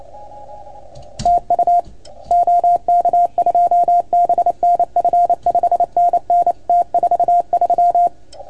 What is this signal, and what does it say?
Morse code, heard as a keyed tone, about 28 wpm, 690 Hz. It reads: TU OK2BNF 5NN T43
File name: Archiv / CUC310 / OK1FGD CUC 310 pileup